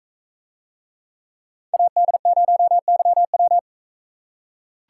Morse code, heard as a keyed tone, about 41 wpm, 700 Hz. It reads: AD0YW